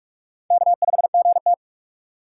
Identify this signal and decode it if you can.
Morse code, heard as a keyed tone, about 44 wpm, 700 Hz. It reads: K5GT
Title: USA Call Signs - 4 Characters - 4x Character Spacing - 45wpm